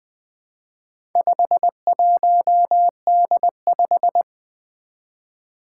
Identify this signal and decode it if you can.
Morse code, heard as a keyed tone, about 20 wpm, 700 Hz. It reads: 51D5
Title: Four Challenging Letter-Number Combinations 20wpm